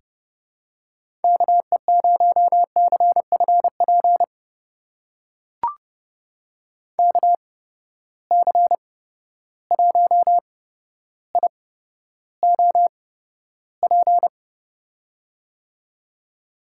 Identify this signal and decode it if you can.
Morse code, heard as a keyed tone, about 29 wpm, spaced as for 13 wpm, 700 Hz. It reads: KE0CFP KC1IOP
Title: USA Call Signs - 6 Characters - 8x Character Spacing - 30wpm